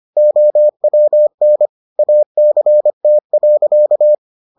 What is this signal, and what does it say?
Morse code, heard as a keyed tone, about 25 wpm, 600 Hz. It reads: OWN ACT.